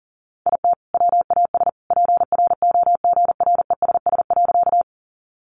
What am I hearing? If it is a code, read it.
Morse code, heard as a keyed tone, about 40 wpm, 700 Hz. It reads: IT WAS PROGRESS.
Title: Sentences from Top 1400 Words 40wpm